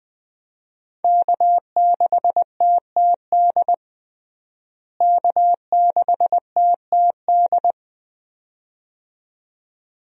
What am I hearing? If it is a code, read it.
Morse code, heard as a keyed tone, about 21 wpm, 700 Hz. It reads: K6TTD K6TTD
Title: USA Call Signs - 5 Characters - Repeated 2x - 20wpm